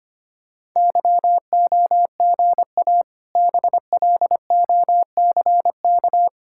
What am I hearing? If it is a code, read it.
Morse code, heard as a keyed tone, about 25 wpm, 700 Hz. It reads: YOGA BLOCK
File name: practice / 25wpm / Difficult Words with Hints - 25wpm